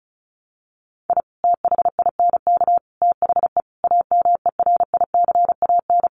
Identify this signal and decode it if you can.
Morse code, heard as a keyed tone, about 34 wpm, 700 Hz. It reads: I THINK THE AMERICAN
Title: Sets of 5 Words Encore 35wpm